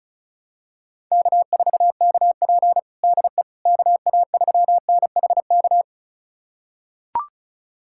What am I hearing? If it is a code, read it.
Morse code, heard as a keyed tone, about 36 wpm, 700 Hz. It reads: K4KP DE KA3NHK
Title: USA Callsign DE Callsign 35wpm